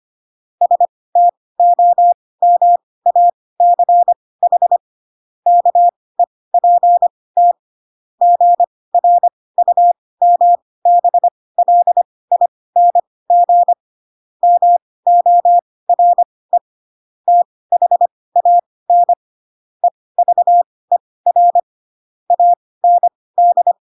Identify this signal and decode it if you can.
Morse code, heard as a keyed tone, about 25 wpm, spaced as for 18 wpm, 700 Hz. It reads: STOMACH KEPT GRUMBLING MORE THAN EVER AND